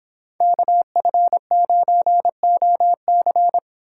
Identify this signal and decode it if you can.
Morse code, heard as a keyed tone, about 26 wpm, 700 Hz. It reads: KF9OC